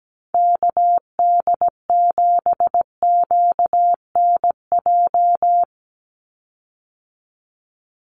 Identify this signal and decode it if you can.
Morse code, heard as a keyed tone, about 17 wpm, 700 Hz. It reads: KD7QNJ
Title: USA Call Signs 17wpm